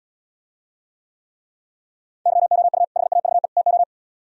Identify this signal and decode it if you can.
Morse code, heard as a keyed tone, about 77 wpm, 700 Hz. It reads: YOU HAVE TO